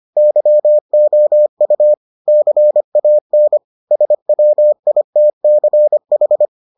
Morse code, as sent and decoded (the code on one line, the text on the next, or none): -.-- --- ..- / -.-. .- -. / ... .-- .. - -.-. ....
YOU CAN SWITCH